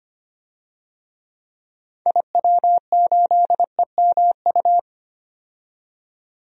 Morse code, heard as a keyed tone, about 25 wpm, 700 Hz. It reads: IW8EMU
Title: DXCC-100 Call Signs - 6 Characters - 25wpm